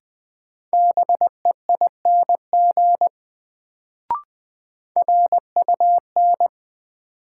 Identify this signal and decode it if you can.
Morse code, heard as a keyed tone, about 20 wpm, 700 Hz. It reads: BEING RUN
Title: Sets of 4 Words - New Word Drill - 20wpm